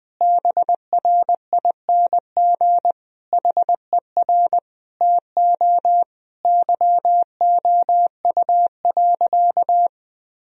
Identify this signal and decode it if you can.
Morse code, heard as a keyed tone, about 21 wpm, 700 Hz. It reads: BRING HER TO YOU.
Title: Sentences from Top 1000 Words - Review Part 3 - 20wpm